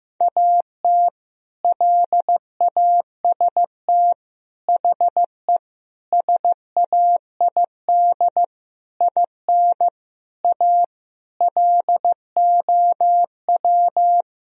Morse code, heard as a keyed tone, about 15 wpm, 700 Hz. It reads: AT LAST HE SAID IN A LOW
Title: Sentences from Top 1000 Words - Review Part 1 - 15wpm